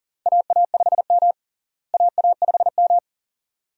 Morse code, heard as a keyed tone, about 40 wpm, 700 Hz. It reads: AA5M AA5M
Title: USA Call Signs - 4 Characters - Repeated 2x - 40wpm